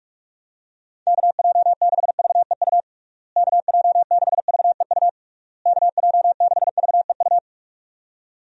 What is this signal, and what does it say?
Morse code, heard as a keyed tone, about 47 wpm, 700 Hz. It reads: KJ6VEU KJ6VEU KJ6VEU
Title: USA Call Signs - 6 Characters - Repeated 3x - 45wpm